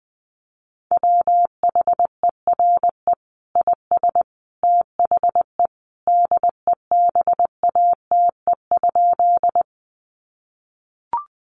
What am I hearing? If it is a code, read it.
Morse code, heard as a keyed tone, about 20 wpm, 700 Hz. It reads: WHERE IS THE DEBATE?
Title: Sentences from Top 1100 Words 20wpm